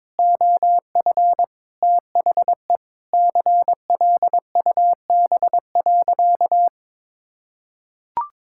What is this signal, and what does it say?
Morse code, heard as a keyed tone, about 22 wpm, 700 Hz. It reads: OF THE CLUB.
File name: practice / 22wpm / Sentences with 3-Word Phrases - 22wpm